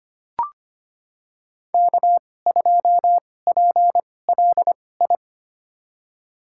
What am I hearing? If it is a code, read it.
Morse code, heard as a keyed tone, about 25 wpm, spaced as for 18 wpm, 700 Hz. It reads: K2PLI
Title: USA Call Signs - 5 Characters - 2x Character Spacing - 25wpm